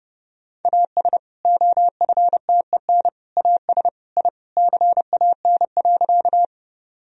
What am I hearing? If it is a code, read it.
Morse code, heard as a keyed tone, about 30 wpm, 700 Hz. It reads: AS OFTEN AS I CAN.
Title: Sentences from Top 1000 Words - Review Part 1 - 30wpm